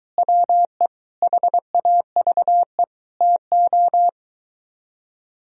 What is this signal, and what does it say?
Morse code, heard as a keyed tone, about 23 wpm, 700 Hz. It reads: WE HAVE TO